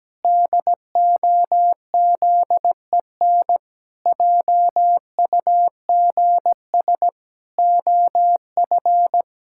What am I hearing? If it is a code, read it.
Morse code, heard as a keyed tone, about 17 wpm, 700 Hz. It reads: DOZEN JUGS OF
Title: Pangram Challenge v2 - 17wpm